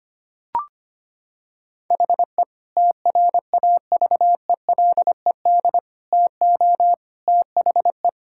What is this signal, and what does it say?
Morse code, heard as a keyed tone, about 25 wpm, 700 Hz. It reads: HE TRAVELED TO THE
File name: practice / 25wpm / Sentences with 3-Word Phrases - 25wpm